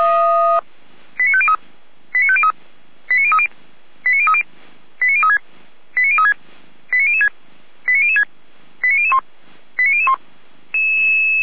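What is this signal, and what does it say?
Five-tone selective call (ZVEI), heard as quick runs of five tones, tones of about 70 ms. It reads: 89503 89503 89030 89030 89036 89036 89006 89006 89001 89001
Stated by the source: ZVEI five-tone call received by SDR, from the feuerwehr-orlamuende.org server.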